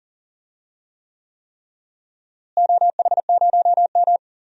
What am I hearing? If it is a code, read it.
Morse code, heard as a keyed tone, about 37 wpm, 700 Hz. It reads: OH0M
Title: DXCC-100 Call Signs - 4 Characters - Repeated 3x - 40wpm